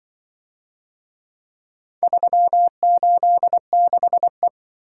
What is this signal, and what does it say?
Morse code, heard as a keyed tone, about 24 wpm, 700 Hz. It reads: 386E
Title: Speed-Racing - Four Letter-Number Combinations - 20wpm